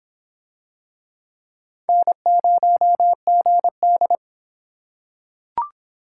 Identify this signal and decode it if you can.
Morse code, heard as a keyed tone, about 26 wpm, 700 Hz. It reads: N0GD